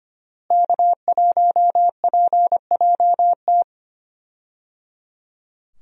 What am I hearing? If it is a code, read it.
Morse code, heard as a keyed tone, about 25 wpm, 700 Hz. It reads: K1PJT
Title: USA Call Signs - 5 Characters - Repeated 3x - 25wpm